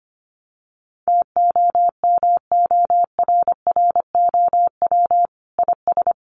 Morse code, heard as a keyed tone, about 25 wpm, 700 Hz. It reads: TOMORROW IS
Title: Additional Easy Phrases 25wpm